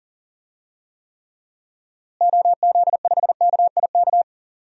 Morse code, heard as a keyed tone, about 40 wpm, 700 Hz. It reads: OZ5KIK
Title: DXCC-100 Call Signs - 6 Characters - Repeated 2x - 40wpm